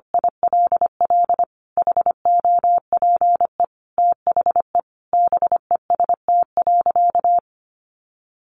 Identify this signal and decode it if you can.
Morse code, heard as a keyed tone, about 25 wpm, 700 Hz. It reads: ILL HOPE THE BEST.
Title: Sentences from Top 1000 Words 25wpm